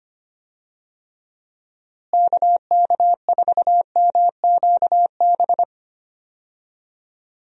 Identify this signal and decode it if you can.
Morse code, heard as a keyed tone, about 25 wpm, 700 Hz. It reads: KK4MQB
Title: USA Call Signs - 6 Characters - Repeated 2x - 25wpm